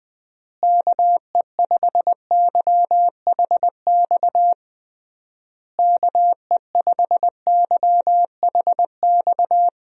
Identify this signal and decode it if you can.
Morse code, heard as a keyed tone, about 21 wpm, 700 Hz. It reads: KE5YHX KE5YHX
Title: USA Call Signs - 6 Characters - Repeated 3x - 20wpm